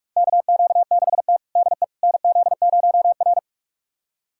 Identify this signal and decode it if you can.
Morse code, heard as a keyed tone, about 44 wpm, 700 Hz. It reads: KQ6T DE NZ0R
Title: USA Callsign DE Callsign - 4 Characters - 45wpm